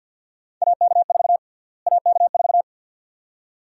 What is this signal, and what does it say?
Morse code, heard as a keyed tone, about 53 wpm, 700 Hz. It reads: AK4 AK4
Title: USA Call Sign Prefixes - Repeated 2x - 50wpm